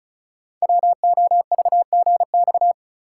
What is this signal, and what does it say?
Morse code, heard as a keyed tone, about 36 wpm, 700 Hz. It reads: WOVGX
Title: Code Groups - 5 Letters - 35wpm